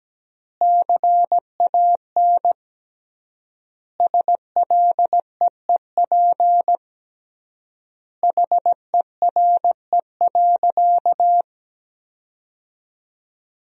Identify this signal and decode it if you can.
Morse code, heard as a keyed tone, about 17 wpm, 700 Hz. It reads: CAN SLEEP HERE.
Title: 3x Word-Spacing - Sentences from Top 500 Words - 17wpm